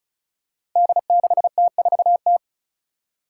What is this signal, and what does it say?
Morse code, heard as a keyed tone, about 36 wpm, 700 Hz. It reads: D6T4T
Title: Five Challenging Letter-Number Combinations 35wpm